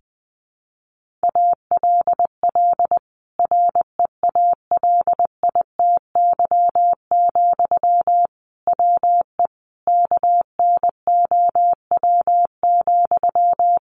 Morse code, heard as a keyed tone, about 20 wpm, 700 Hz. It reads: ALL REALITY, WE KNOW,
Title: Sentences from Top 1400 Words 20wpm